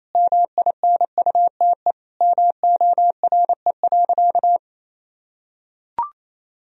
Morse code, heard as a keyed tone, about 28 wpm, 700 Hz. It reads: MINUTE MORE.
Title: Sentences from Top 1000 Words - Review Part 4 - 28wpm